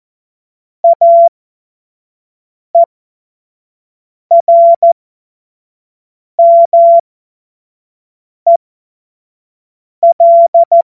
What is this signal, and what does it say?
Morse code, heard as a keyed tone, about 14 wpm, spaced as for 5 wpm, 680 Hz. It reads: AERMEL
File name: Morsekurs / Witze / Witz 08. Februar 2022 25 F70